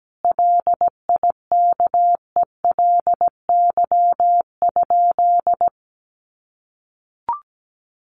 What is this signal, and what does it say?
Morse code, heard as a keyed tone, about 17 wpm, 700 Hz. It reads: LIKELY?
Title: Sentences from Top 1100 Words 17wpm